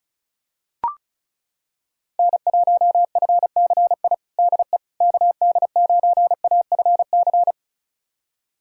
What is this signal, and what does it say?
Morse code, heard as a keyed tone, about 34 wpm, 700 Hz. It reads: N1FCI DE KD9AFC
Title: USA Callsign DE Callsign 35wpm